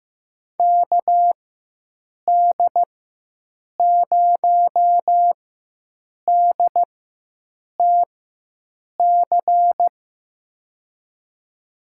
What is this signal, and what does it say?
Morse code, heard as a keyed tone, about 15 wpm, spaced as for 7 wpm, 700 Hz. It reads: KD0DTC
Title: USA Call Signs - 6 Characters - 4x Character Spacing - 15wpm